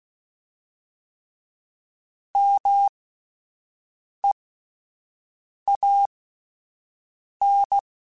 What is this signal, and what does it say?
Morse code, heard as a keyed tone, about 16 wpm, spaced as for 5 wpm, 800 Hz. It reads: MEAN
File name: Alphabetical / Alphabetical mp3s / Ms @5